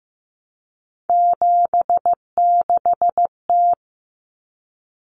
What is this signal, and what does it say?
Morse code, heard as a keyed tone, about 15 wpm, 700 Hz. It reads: 76T
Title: Common Cut Numbers - 3-Digit 15wpm